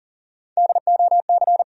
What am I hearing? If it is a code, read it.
Morse code, heard as a keyed tone, about 40 wpm, 700 Hz. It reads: DOC
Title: USA Call Sign Postfixes - 3 Letters 40wpm